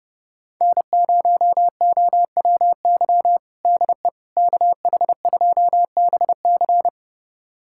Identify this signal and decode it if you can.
Morse code, heard as a keyed tone, about 32 wpm, 700 Hz. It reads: N0OWY DE KH2BC
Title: USA Callsign DE Callsign - 5 Characters - 30wpm